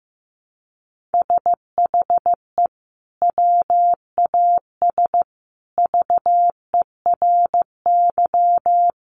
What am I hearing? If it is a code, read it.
Morse code, heard as a keyed tone, about 15 wpm, 700 Hz. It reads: SHE WAS VERY
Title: Sentences from Top 900 Words 15wpm